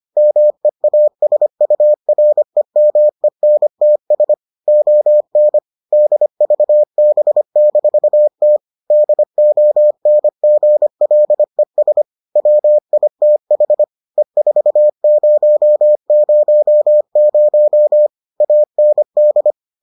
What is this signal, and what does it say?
Morse code, heard as a keyed tone, about 25 wpm, 600 Hz. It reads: MEASUREMENTS ON DVB-T DONGLES WITH E4000 AND